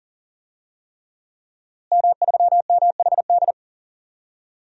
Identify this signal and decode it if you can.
Morse code, heard as a keyed tone, about 37 wpm, 700 Hz. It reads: M3MHD
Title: DXCC-100 Call Signs - 5 Characters - Repeated 3x - 40wpm